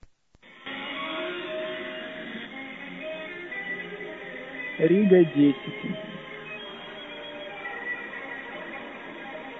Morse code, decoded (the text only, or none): none